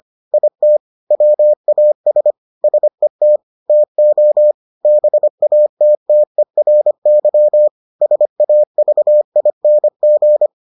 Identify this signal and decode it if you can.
Morse code, heard as a keyed tone, about 25 wpm, 600 Hz. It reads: IT WAS SET TO BATTERY SAVING